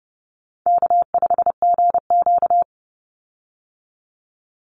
Morse code, heard as a keyed tone, about 30 wpm, 700 Hz. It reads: K5GQ
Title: Call Signs Repeated 3x - 30wpm